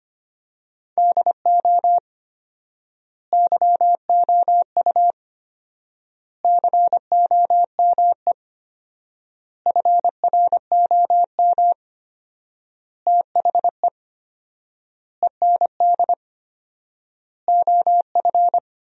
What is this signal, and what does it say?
Morse code, heard as a keyed tone, about 25 wpm, 700 Hz. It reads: DO YOU COME FROM THE END OF
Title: 4x Word-Spacing - Sentences from Top 200 Words - 25wpm